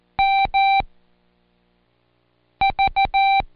Morse code, heard as a keyed tone, about 14 wpm, spaced as for 4 wpm, 780 Hz. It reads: MV
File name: Code Quick / AudioDownld1 / Seg63-Day7-Pract PeriodMV